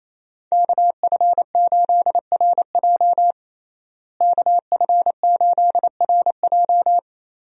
Morse code, heard as a keyed tone, about 27 wpm, 700 Hz. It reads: KF8RJ KF8RJ
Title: USA Call Signs - 5 Characters - Repeated 3x - 28wpm